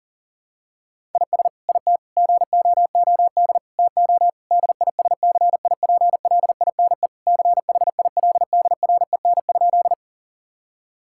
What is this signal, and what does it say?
Morse code, heard as a keyed tone, about 40 wpm, 700 Hz. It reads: IS IT GOOD TO DISCIPLINE CHILDREN?
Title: Sentences from Top 1000 Words - Review Part 6 - 40wpm